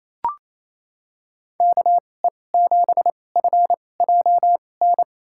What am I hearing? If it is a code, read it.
Morse code, heard as a keyed tone, about 28 wpm, spaced as for 20 wpm, 700 Hz. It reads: KE7FJN
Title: USA Call Signs - 6 Characters - 2x Character Spacing - 28wpm